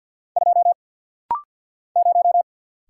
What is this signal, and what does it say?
Morse code, heard as a keyed tone, about 48 wpm, 690 Hz. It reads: 2 0
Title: Single Letter-Number - Rapid-Fire 50wpm